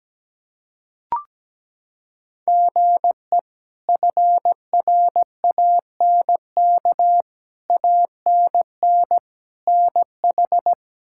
Morse code, heard as a keyed tone, about 17 wpm, 700 Hz. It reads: GE FRANK ANN NH
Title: K1USN SST QSO Practice - 17wpm